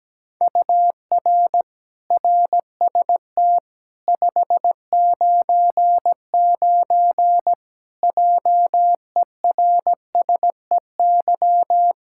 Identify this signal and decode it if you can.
Morse code, heard as a keyed tone, about 17 wpm, 700 Hz. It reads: UR RST 599 JERSEY
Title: Practice QSO - Exchange 1 - Beginning - 17wpm